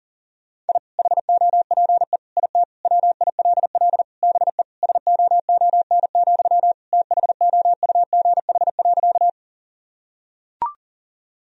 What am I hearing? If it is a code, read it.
Morse code, heard as a keyed tone, about 37 wpm, 700 Hz. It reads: I HOPE IT WILL BE SOON, THOUGH.
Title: Sentences from Top 1000 Words - Review Part 2 - 40wpm